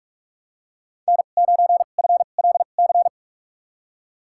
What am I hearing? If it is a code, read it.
Morse code, heard as a keyed tone, about 43 wpm, spaced as for 32 wpm, 700 Hz. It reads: N9FLC
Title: USA Call Signs - 5 Characters - 2x Character Spacing - 45wpm